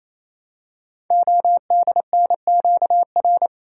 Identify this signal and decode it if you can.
Morse code, heard as a keyed tone, about 27 wpm, 700 Hz. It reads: ODNQR
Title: Code Groups - 5 Letters - 28wpm